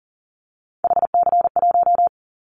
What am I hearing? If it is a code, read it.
Morse code, heard as a keyed tone, about 41 wpm, 700 Hz. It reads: HC1
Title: Callsign Prefixes for Top 100 DXCC - 40wpm